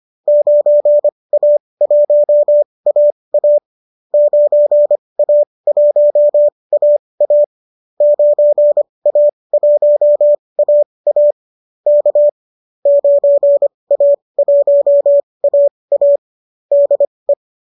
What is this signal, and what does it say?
Morse code, heard as a keyed tone, about 25 wpm, spaced as for 20 wpm, 590 Hz. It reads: 9A1AA 9A1AA 9A1AA K 9A1AA DE
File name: qso219 20